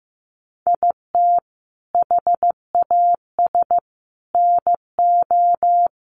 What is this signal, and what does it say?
Morse code, heard as a keyed tone, about 15 wpm, 700 Hz. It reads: IT HAS NO
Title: Sentences from Top 1200 Words 15wpm